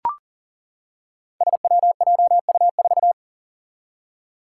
Morse code, heard as a keyed tone, about 38 wpm, 700 Hz. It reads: SWJU4
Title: Five Challenging Letter-Number Combinations 40wpm